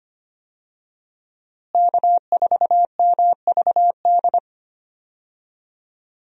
Morse code, heard as a keyed tone, about 25 wpm, 700 Hz. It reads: K4MVD
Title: USA Call Signs - 5 Characters - Repeated 2x - 25wpm